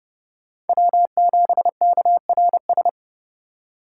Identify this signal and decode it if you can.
Morse code, heard as a keyed tone, about 32 wpm, 700 Hz. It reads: W7KRS
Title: K1USN SST QSO Practice - 30wpm